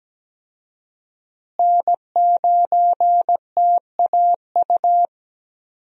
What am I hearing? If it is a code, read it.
Morse code, heard as a keyed tone, about 17 wpm, 700 Hz. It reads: N9TAU
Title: USA Call Signs - 5 Characters - Repeated 2x - 17wpm